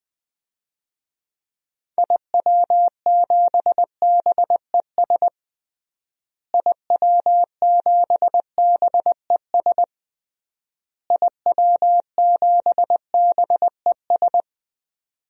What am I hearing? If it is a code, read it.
Morse code, heard as a keyed tone, about 21 wpm, 700 Hz. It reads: IW7BES IW7BES IW7BES
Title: DXCC-100 Call Signs - 6 Characters - Repeated 3x - 20wpm